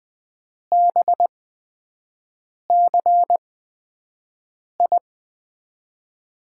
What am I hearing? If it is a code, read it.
Morse code, heard as a keyed tone, about 20 wpm, spaced as for 5 wpm, 700 Hz. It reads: BCI
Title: USA Call Signs - 6 Characters - 8x Character Spacing - 20wpm